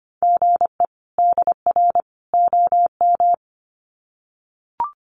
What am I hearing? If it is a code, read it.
Morse code, heard as a keyed tone, about 25 wpm, 700 Hz. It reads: GE DR OM
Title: Common QSO Phrases 25wpm